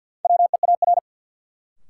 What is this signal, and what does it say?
Morse code, heard as a keyed tone, about 53 wpm, 700 Hz. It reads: WEAR
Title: Sentences from Top 1500 Words - New Word Drill Alt - 50wpm